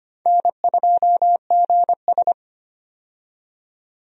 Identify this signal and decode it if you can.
Morse code, heard as a keyed tone, about 25 wpm, 700 Hz. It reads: N2GS
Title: USA Call Signs - 4 Characters - 25wpm